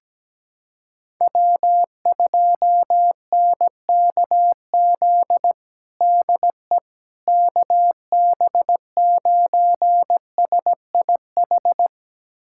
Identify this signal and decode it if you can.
Morse code, heard as a keyed tone, about 17 wpm, 700 Hz. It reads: W2NKZ DE KB9SIH
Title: USA Callsign DE Callsign 17wpm